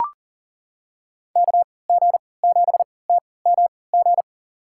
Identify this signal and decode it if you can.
Morse code, heard as a keyed tone, about 40 wpm, spaced as for 22 wpm, 700 Hz. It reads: KG7TMG
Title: USA Call Signs - 6 Characters - 3x Character Spacing - 40wpm